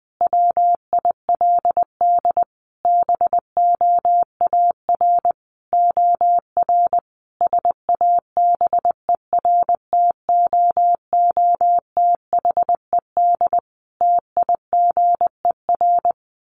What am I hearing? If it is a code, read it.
Morse code, heard as a keyed tone, about 20 wpm, 700 Hz. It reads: WILD BOAR OR SABERTOOTHED TIGER
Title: A Sound Of Thunder 20wpm